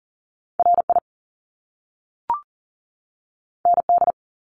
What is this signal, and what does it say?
Morse code, heard as a keyed tone, about 40 wpm, 700 Hz. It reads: RI ND
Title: Top Two Letter Combinations in Words 40wpm